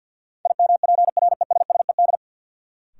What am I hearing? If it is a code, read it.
Morse code, heard as a keyed tone, about 52 wpm, 700 Hz. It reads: IMPRESSED
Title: Sentences from Top 1600 Words - New Word Drill - 50wpm